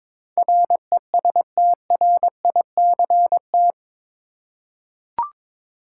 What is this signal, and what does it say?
Morse code, heard as a keyed tone, about 22 wpm, 700 Hz. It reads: RESTRICT
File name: practice / 22wpm / Eight Letter Words - 22wpm